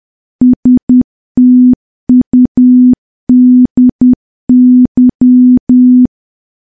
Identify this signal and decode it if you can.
Morse code, heard as a keyed tone, about 10 wpm, 250 Hz. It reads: STUDY
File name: morse / morse-training-10WPM